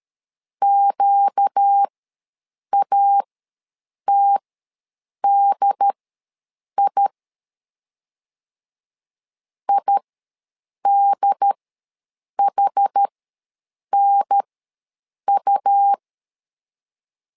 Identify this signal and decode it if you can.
Morse code, heard as a keyed tone, about 13 wpm, spaced as for 7 wpm, 800 Hz. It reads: QATDI IDHNU
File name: LEZIONI / 4 Q ws 12wpm 1912